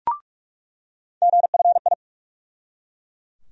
Morse code, heard as a keyed tone, about 43 wpm, 700 Hz. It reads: GFI
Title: USA Call Sign Postfixes - 3 Letters 45wpm